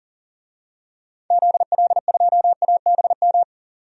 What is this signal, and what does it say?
Morse code, heard as a keyed tone, about 40 wpm, 700 Hz. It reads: ZL2ABM
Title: DXCC-100 Call Signs - 6 Characters - 40wpm